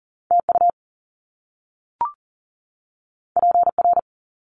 Morse code, heard as a keyed tone, about 40 wpm, 700 Hz. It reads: TU PR
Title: Top Two Letter Combinations in Words 40wpm